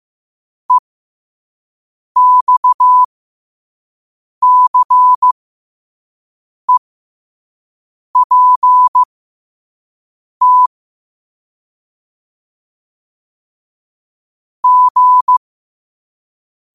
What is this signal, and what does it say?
Morse code, heard as a keyed tone, about 15 wpm, spaced as for 5 wpm, 1000 Hz. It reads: EXCEPT G
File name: qotdfiles / 5wpm